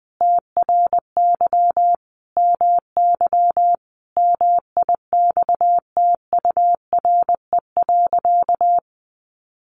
Morse code, heard as a keyed tone, about 20 wpm, 700 Hz. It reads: TRY MY MIXTURE.
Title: Sentences from Top 1500 Words 20wpm